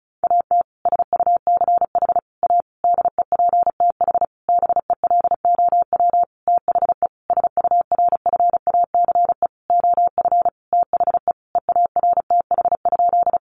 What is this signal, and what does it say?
Morse code, heard as a keyed tone, about 34 wpm, 700 Hz. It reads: AT SUCH A DEPTH BELOW THE SURFACE OF THE EARTH?
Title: Sentences from Top 1200 Words 35wpm